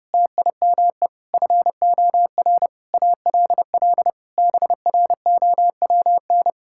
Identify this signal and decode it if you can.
Morse code, heard as a keyed tone, about 32 wpm, 700 Hz. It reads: TIME FOR ALL BROWN
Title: Pangram Challenge v2 - 30wpm